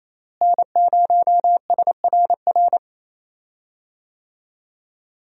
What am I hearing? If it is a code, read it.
Morse code, heard as a keyed tone, about 28 wpm, 700 Hz. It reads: N0SRR
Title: USA Call Signs - 5 Characters - Repeated 2x - 28wpm